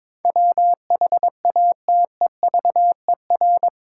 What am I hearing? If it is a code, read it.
Morse code, heard as a keyed tone, about 22 wpm, 700 Hz. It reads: WHATEVER
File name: practice / 22wpm / Eight Letter Words - 22wpm